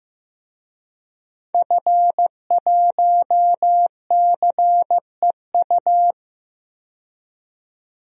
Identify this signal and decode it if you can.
Morse code, heard as a keyed tone, about 16 wpm, 700 Hz. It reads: F1CEU
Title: DXCC-100 Call Signs - 5 Characters - 15wpm